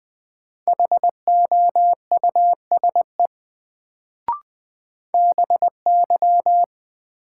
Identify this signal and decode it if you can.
Morse code, heard as a keyed tone, about 21 wpm, 700 Hz. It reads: HOUSE BY
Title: Sets of 2 Words - New Word Drill - 20wpm